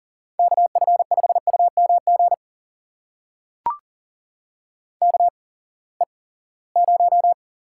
Morse code, heard as a keyed tone, about 40 wpm, spaced as for 23 wpm, 700 Hz. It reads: KF5UMG KE0
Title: USA Call Signs - 6 Characters - 8x Character Spacing - 40wpm